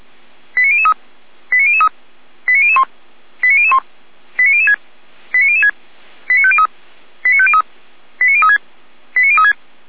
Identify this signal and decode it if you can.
Five-tone selective call (ZVEI), heard as quick runs of five tones, tones of about 70 ms. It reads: 89003 89003 89001 89001 89006 89006 89503 89503 89036 89036
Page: /455/SDR/ZVEI/89160/